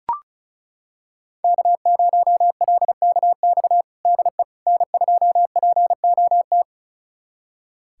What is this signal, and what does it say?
Morse code, heard as a keyed tone, about 36 wpm, 700 Hz. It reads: K0LKX DE N2POT
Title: USA Callsign DE Callsign - 5 Characters - 35wpm